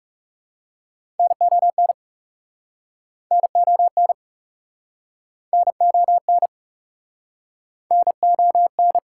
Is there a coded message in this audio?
Yes: Morse code from 1.2 to 9.0 s, heard as a keyed tone, about 43 wpm, 690 Hz. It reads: NON NON NON NON